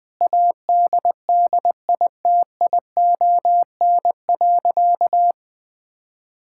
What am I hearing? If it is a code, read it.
Morse code, heard as a keyed tone, about 20 wpm, 700 Hz. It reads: ADDITION.
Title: Sentences from Top 1000 Words - Review Part 6 - 20wpm